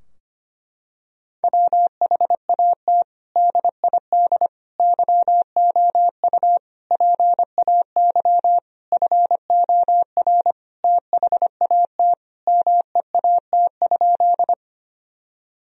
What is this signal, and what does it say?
Morse code, heard as a keyed tone, about 25 wpm, 700 Hz. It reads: WHAT DID YOU PAY FOR THAT MEAT?
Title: Sentences from Top 1000 Words - Review Part 3 - 25wpm